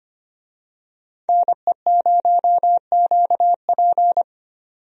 Morse code, heard as a keyed tone, about 25 wpm, 700 Hz. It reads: NE0QP
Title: USA Call Signs - 5 Characters - 25wpm